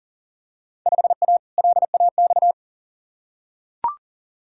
Morse code, heard as a keyed tone, about 40 wpm, 700 Hz. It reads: 5A LAX
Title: Field Day Exchanges - 40wpm